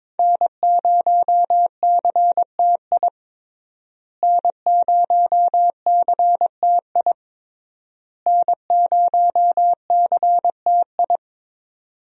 Morse code, heard as a keyed tone, about 22 wpm, 700 Hz. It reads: N0CTI N0CTI N0CTI
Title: USA Call Signs - 5 Characters - Repeated 3x - 22wpm